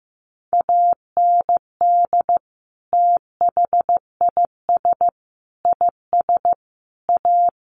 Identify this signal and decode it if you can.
Morse code, heard as a keyed tone, about 15 wpm, 700 Hz. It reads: AND THIS IS A
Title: Sentences from Top 1400 Words 15wpm